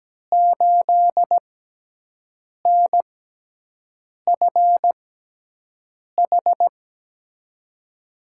Morse code, heard as a keyed tone, about 17 wpm, spaced as for 6 wpm, 700 Hz. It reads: 8NFH
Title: USA Call Signs - 6x Character Spacing - 17wpm